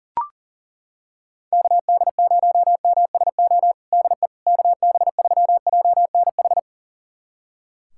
Morse code, heard as a keyed tone, about 40 wpm, 700 Hz. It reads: KD0MSO DE KB3JNH
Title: USA Callsign DE Callsign - 6 Characters - 40wpm